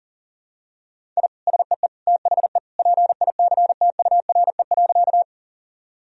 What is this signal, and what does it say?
Morse code, heard as a keyed tone, about 40 wpm, 700 Hz. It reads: I SEE THE PICTURE.
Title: Sentences from Top 1000 Words - Review Part 1 - 40wpm